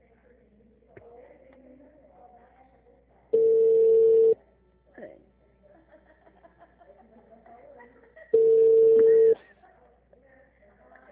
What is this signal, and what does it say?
Ringback tone (China, 450 Hz), with speech.
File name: out- -2002- - - .190